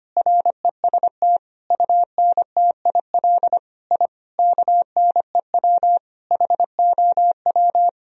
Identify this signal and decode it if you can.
Morse code, heard as a keyed tone, about 25 wpm, 700 Hz. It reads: REST UNTIL I KNEW HOW
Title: Sentences from Top 1000 Words - Review Part 1 - 25wpm